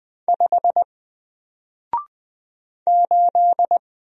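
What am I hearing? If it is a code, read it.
Morse code, heard as a keyed tone, about 20 wpm, 700 Hz. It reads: 5 8
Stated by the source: Single letter and number Morse drill at 20 wpm, from N4LCD.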